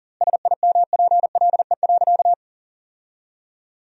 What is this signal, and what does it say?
Morse code, heard as a keyed tone, about 40 wpm, 700 Hz. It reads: SIMPLE.
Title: 2x Word-Spacing - Sentences from Top 400 Words - 40wpm